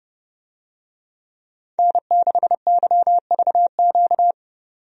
Morse code, heard as a keyed tone, about 29 wpm, 700 Hz. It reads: N6YVQ
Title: USA Call Signs - 5 Characters - 30wpm